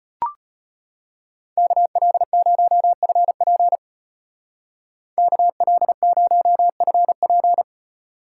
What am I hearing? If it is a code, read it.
Morse code, heard as a keyed tone, about 32 wpm, 700 Hz. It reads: KL0FP KL0FP